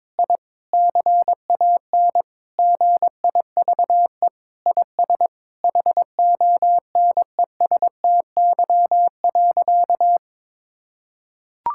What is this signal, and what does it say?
Morse code, heard as a keyed tone, about 22 wpm, 700 Hz. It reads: I CAN GIVE IS HONESTY.
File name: practice / 22wpm / Sentences with 8-Word Phrases - 22wpm